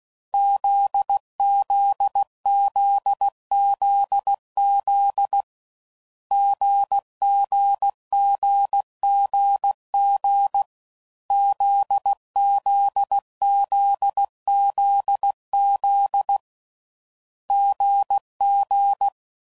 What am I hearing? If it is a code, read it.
Morse code, heard as a keyed tone, about 16 wpm, 800 Hz. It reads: ZZZZZ GGGGG ZZZZZ GG
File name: Combinations / Difficult Combos / GZ GZ7